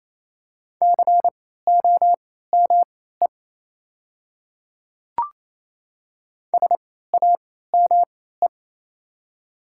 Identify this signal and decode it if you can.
Morse code, heard as a keyed tone, about 27 wpm, spaced as for 16 wpm, 700 Hz. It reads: COME SAME
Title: 3x Character Spacing - Top 200 Words - 28wpm